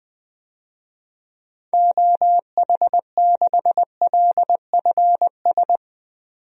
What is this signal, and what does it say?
Morse code, heard as a keyed tone, about 21 wpm, 700 Hz. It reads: OH6LFS
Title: DXCC-100 Call Signs - 6 Characters - 20wpm